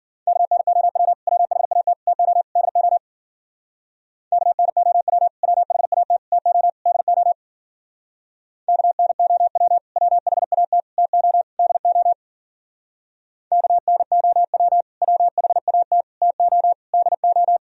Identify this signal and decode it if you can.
Morse code, heard as a keyed tone, about 51 wpm, 700 Hz. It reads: KNOW WHAT TO DO KNOW WHAT TO DO KNOW WHAT TO DO KNOW WHAT TO DO